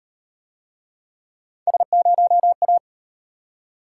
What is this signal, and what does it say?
Morse code, heard as a keyed tone, about 37 wpm, 700 Hz. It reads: S0A